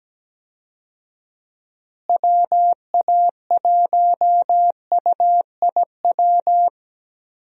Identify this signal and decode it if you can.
Morse code, heard as a keyed tone, about 17 wpm, 700 Hz. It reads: WA1UIW
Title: USA Call Signs - 6 Characters - Repeated 3x - 17wpm